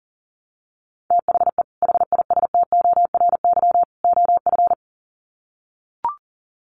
Morse code, heard as a keyed tone, about 40 wpm, 700 Hz. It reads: THE HISTORY OF
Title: Sets of 3 Words 40wpm